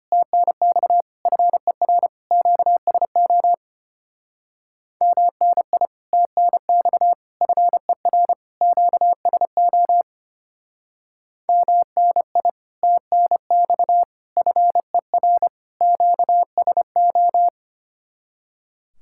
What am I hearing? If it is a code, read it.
Morse code, heard as a keyed tone, about 32 wpm, 700 Hz. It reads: TNX FER QSO MNI TNX FER QSO MNI TNX FER QSO